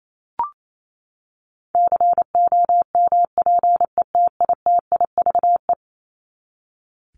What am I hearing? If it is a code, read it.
Morse code, heard as a keyed tone, about 28 wpm, 700 Hz. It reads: COMPETITIVE
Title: Top 1001-1250 Words 28wpm